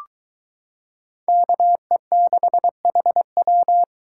Morse code, heard as a keyed tone, about 22 wpm, 700 Hz. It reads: KE6HW